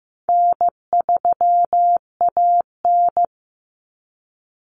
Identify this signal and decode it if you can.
Morse code, heard as a keyed tone, about 15 wpm, 700 Hz. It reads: N3AN
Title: Common Cut Numbers - 4-Digit 15wpm